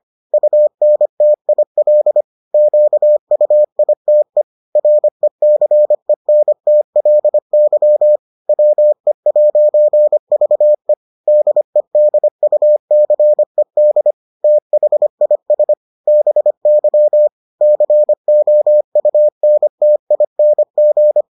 Morse code, heard as a keyed tone, about 25 wpm, 600 Hz. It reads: UNTIL QUITE RECENTLY WE'VE DEDUCED THIS BY COUNTING